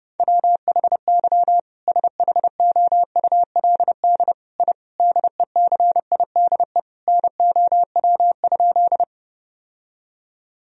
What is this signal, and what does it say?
Morse code, heard as a keyed tone, about 30 wpm, 700 Hz. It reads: WHY SHOULD I DECIDE NOW?
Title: Sentences from Top 1000 Words - Review Part 3 - 30wpm